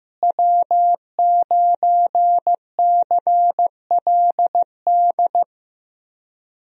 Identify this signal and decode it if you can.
Morse code, heard as a keyed tone, about 15 wpm, 700 Hz. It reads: W9CLD
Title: USA Call Signs - 5 Characters - Repeated 2x - 15wpm